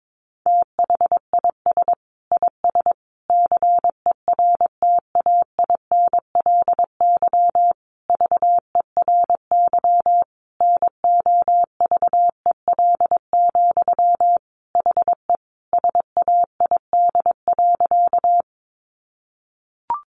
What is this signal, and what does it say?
Morse code, heard as a keyed tone, about 22 wpm, 700 Hz. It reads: THIS IS CERTAINLY VERY NOVEL, HE SAID.
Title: Sentences from Top 1300 Words 22wpm